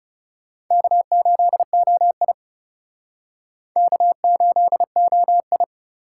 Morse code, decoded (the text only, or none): K8OI K8OI